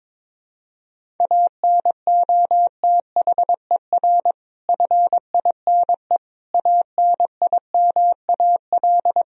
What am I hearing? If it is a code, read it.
Morse code, heard as a keyed tone, about 22 wpm, 700 Hz. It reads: ANOTHER FINE ANIMAL
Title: Sentences from Top 1000 Words - Review Part 1 - 22wpm